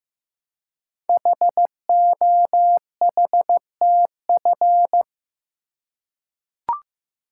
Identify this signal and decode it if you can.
Morse code, heard as a keyed tone, about 15 wpm, 700 Hz. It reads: HOHTF
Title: Code Groups - 5 Letters - 15wpm